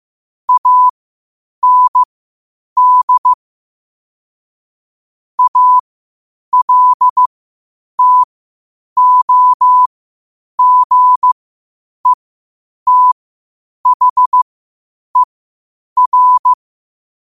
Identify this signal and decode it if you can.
Morse code, heard as a keyed tone, about 15 wpm, spaced as for 8 wpm, 1000 Hz. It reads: AND ALTOGETHER
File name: qotdfiles / 8wpm